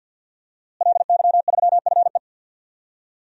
Morse code, heard as a keyed tone, about 49 wpm, 700 Hz. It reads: LY3FE
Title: DXCC-100 Call Signs - 5 Characters - Repeated 2x - 50wpm